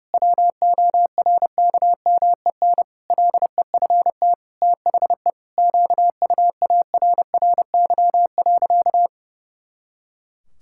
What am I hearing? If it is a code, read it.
Morse code, heard as a keyed tone, about 30 wpm, 700 Hz. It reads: WORKMEN LEFT THE QUARRY.
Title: Pangram Challenge v2 - 30wpm